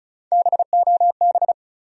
Morse code, heard as a keyed tone, about 34 wpm, 700 Hz. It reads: BOB
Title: USA Common Names - 35wpm